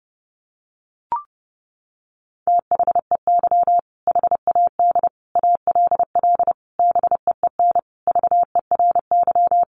Morse code, heard as a keyed tone, about 30 wpm, 700 Hz. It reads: THEY HAD ALL BEEN VERY
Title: Sentences from Top 200 Words 30wpm